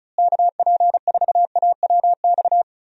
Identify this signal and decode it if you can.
Morse code, heard as a keyed tone, about 36 wpm, 700 Hz. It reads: KP4AWX
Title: DXCC-100 Call Signs - 6 Characters - Repeated 3x - 35wpm